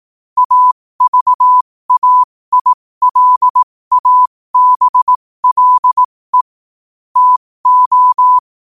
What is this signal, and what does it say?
Morse code, heard as a keyed tone, about 18 wpm, spaced as for 15 wpm, 1000 Hz. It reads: AVAILABLE TO